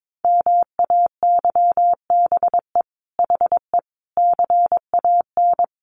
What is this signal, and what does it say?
Morse code, heard as a keyed tone, about 22 wpm, 700 Hz. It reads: MAYBE HE CAN
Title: Sentences from Top 1100 Words 22wpm